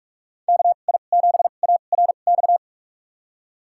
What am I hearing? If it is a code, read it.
Morse code, heard as a keyed tone, about 44 wpm, spaced as for 32 wpm, 700 Hz. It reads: KI7ARX
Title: USA Call Signs - 6 Characters - 2x Character Spacing - 45wpm